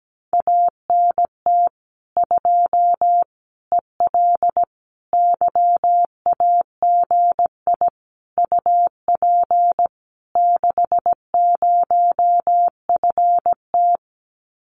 Morse code, heard as a keyed tone, about 17 wpm, 700 Hz. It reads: ANT 2 EL YAGI UP 60FT
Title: Common QSO Phrases Encore 17wpm